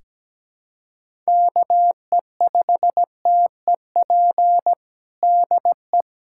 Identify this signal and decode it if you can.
Morse code, heard as a keyed tone, about 17 wpm, 700 Hz. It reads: KE5TEP DE
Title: USA Callsign DE Callsign - 6 Characters - 17wpm